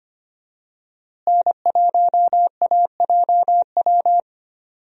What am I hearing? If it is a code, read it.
Morse code, heard as a keyed tone, about 25 wpm, 700 Hz. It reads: N1AJW
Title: USA Call Signs - 5 Characters - Repeated 2x - 25wpm